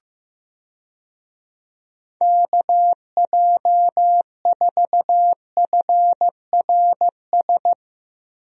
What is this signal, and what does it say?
Morse code, heard as a keyed tone, about 15 wpm, 700 Hz. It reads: KJ4FRS
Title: USA Call Signs - 6 Characters - Repeated 3x - 15wpm